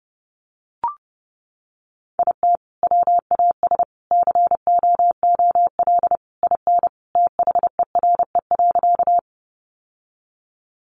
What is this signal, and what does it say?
Morse code, heard as a keyed tone, about 30 wpm, 700 Hz. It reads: IT WAS COOL IN THERE.
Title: Sentences from Top 1200 Words 30wpm